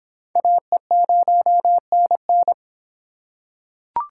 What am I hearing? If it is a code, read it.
Morse code, heard as a keyed tone, about 25 wpm, 700 Hz. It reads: AE0NN